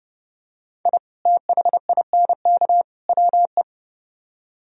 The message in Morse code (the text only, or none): I THINK WE